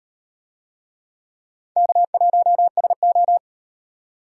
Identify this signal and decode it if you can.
Morse code, heard as a keyed tone, about 38 wpm, 700 Hz. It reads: K1SO